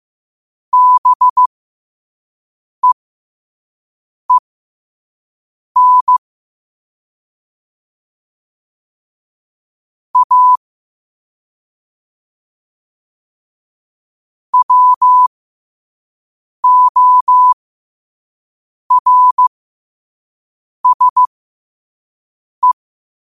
Morse code, heard as a keyed tone, about 15 wpm, spaced as for 5 wpm, 1000 Hz. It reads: BEEN A WORSE
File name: qotdfiles / 5wpm